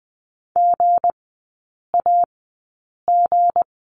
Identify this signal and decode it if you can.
Morse code, heard as a keyed tone, about 20 wpm, spaced as for 8 wpm, 690 Hz. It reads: GAG
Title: 159 Lesson 26e - Letter g - Three Characters 20wpm